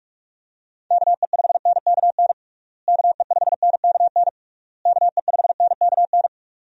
Morse code, heard as a keyed tone, about 46 wpm, 700 Hz. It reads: KE5NKN KE5NKN KE5NKN
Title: USA Call Signs - 6 Characters - Repeated 3x - 45wpm